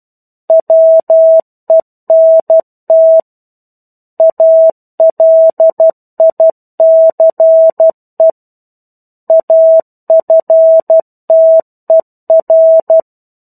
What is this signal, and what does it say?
Morse code, heard as a keyed tone, about 12 wpm, 650 Hz. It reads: WENT ALICE AFTER
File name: 01-Chapter-01 12 WPM